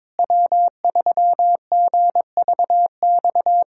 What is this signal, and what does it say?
Morse code, heard as a keyed tone, about 22 wpm, 700 Hz. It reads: W3GVX
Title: K1USN SST QSO Practice - 22wpm